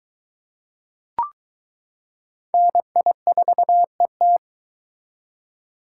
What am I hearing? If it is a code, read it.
Morse code, heard as a keyed tone, about 22 wpm, 700 Hz. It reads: NI4ET